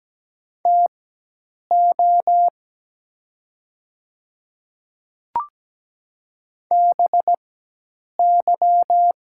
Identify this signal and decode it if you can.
Morse code, heard as a keyed tone, about 17 wpm, spaced as for 8 wpm, 700 Hz. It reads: TO BY
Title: 4x Character Spacing - Top 100 Words - 17wpm